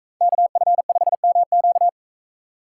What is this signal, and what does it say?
Morse code, heard as a keyed tone, about 40 wpm, 700 Hz. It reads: KF5MQ